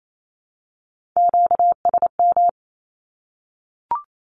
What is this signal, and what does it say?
Morse code, heard as a keyed tone, about 28 wpm, 700 Hz. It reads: QSM
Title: Q-Codes Repeated 3x 28wpm